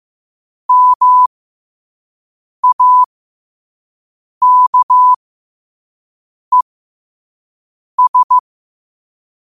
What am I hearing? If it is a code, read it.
Morse code, heard as a keyed tone, about 15 wpm, spaced as for 5 wpm, 1000 Hz. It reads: MAKES
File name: qotdfiles / 5wpm